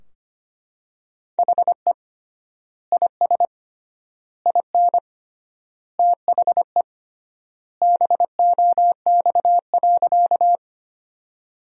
Morse code, heard as a keyed tone, about 25 wpm, 700 Hz. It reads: HE IS IN THE BOX.
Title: 3x Word-Spacing - Sentences from Top 500 Words - 25wpm